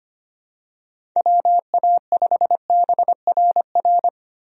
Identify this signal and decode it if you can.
Morse code, heard as a keyed tone, about 25 wpm, 700 Hz. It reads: WA5BRR
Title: USA Call Signs - 6 Characters - 25wpm